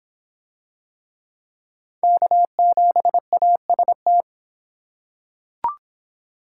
Morse code, heard as a keyed tone, about 24 wpm, 700 Hz. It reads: K7AST